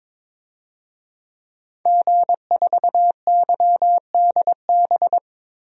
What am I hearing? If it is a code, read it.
Morse code, heard as a keyed tone, about 21 wpm, 700 Hz. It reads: G4YDB
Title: DXCC-100 Call Signs - 5 Characters - Repeated 2x - 22wpm